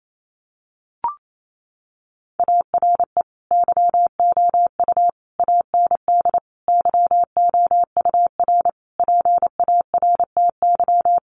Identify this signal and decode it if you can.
Morse code, heard as a keyed tone, about 28 wpm, 700 Hz. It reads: ARE YOU AND YOUR PARTY
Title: Sentences from Top 1400 Words 28wpm